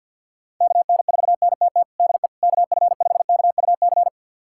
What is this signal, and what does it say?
Morse code, heard as a keyed tone, about 50 wpm, 700 Hz. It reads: KN4NTT DE KF5KUC
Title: USA Callsign DE Callsign 50wpm